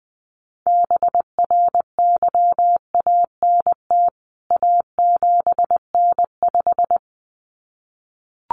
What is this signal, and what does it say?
Morse code, heard as a keyed tone, about 20 wpm, 700 Hz. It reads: BRYANT A7N5
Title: CWT Exchange 20wpm